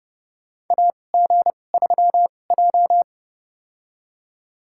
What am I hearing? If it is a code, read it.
Morse code, heard as a keyed tone, about 30 wpm, spaced as for 21 wpm, 700 Hz. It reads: AG3J
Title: USA Call Signs - 4 Characters - 2x Character Spacing - 30wpm